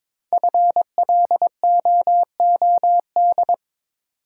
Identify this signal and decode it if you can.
Morse code, heard as a keyed tone, about 22 wpm, 700 Hz. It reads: FLOOD
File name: practice / 22wpm / 5 Letter Words - First 3 Letters Given - 22wpm